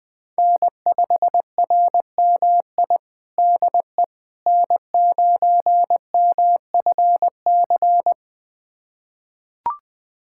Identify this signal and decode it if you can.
Morse code, heard as a keyed tone, about 20 wpm, 700 Hz. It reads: N5RMI DE N9MFC
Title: USA Callsign DE Callsign - 5 Characters - 20wpm